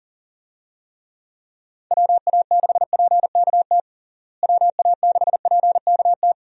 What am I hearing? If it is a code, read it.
Morse code, heard as a keyed tone, about 37 wpm, 700 Hz. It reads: WA6PKT WA6PKT
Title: USA Call Signs - 6 Characters - Repeated 2x - 40wpm